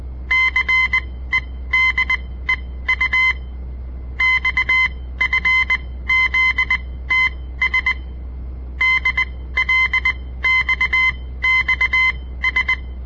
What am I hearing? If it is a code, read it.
Morse code, heard as a keyed tone, about 20 wpm, spaced as for 15 wpm, 2050 Hz. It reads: CEDEU XFZTS DLXXS
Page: On the air Morse code practice from the Southern Montana Amatuer Radio Association; all code, no read back